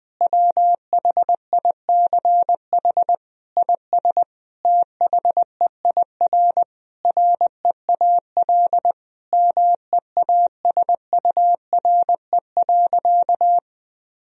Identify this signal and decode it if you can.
Morse code, heard as a keyed tone, about 19 wpm, 700 Hz. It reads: WHICH IS THEIR REAL MEASURE.
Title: Sentences from Top 1000 Words - Review Part 1 - 20wpm